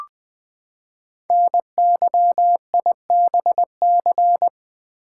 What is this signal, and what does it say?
Morse code, heard as a keyed tone, about 20 wpm, 700 Hz. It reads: NYIBC
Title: Code Groups - 5 Letter:Numbers - 20wpm